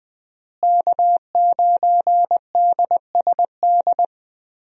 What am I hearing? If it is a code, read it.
Morse code, heard as a keyed tone, about 21 wpm, 700 Hz. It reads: K9DSD
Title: USA Call Signs - 5 Characters - Repeated 3x - 20wpm